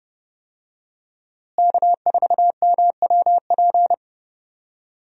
Morse code, heard as a keyed tone, about 30 wpm, 700 Hz. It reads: K4MWP
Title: USA Call Signs - 5 Characters - Repeated 3x - 30wpm